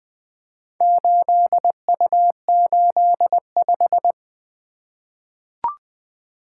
Morse code, heard as a keyed tone, about 20 wpm, 700 Hz. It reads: 8U85
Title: Four Letter-Number Combinations - 20wpm